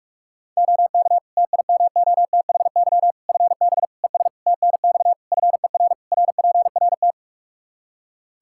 Morse code, heard as a keyed tone, about 41 wpm, 700 Hz. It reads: OK TIMOTHY FB ES TNX FER RPRT
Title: K1USN SST QSO Practice 45wpm